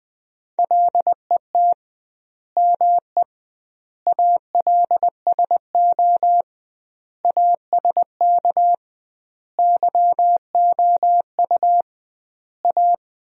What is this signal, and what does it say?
Morse code, heard as a keyed tone, about 21 wpm, 700 Hz. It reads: LET ME ALSO ASK YOU A
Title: 2x Word-Spacing - Sentences from Top 400 Words - 20wpm